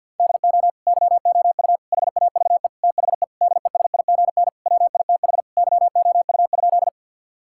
Morse code, heard as a keyed tone, about 52 wpm, 700 Hz. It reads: DO YOU HAVE THE DESIGN WITH YOU?
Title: Sentences from Top 1000 Words - Review Part 5 - 50wpm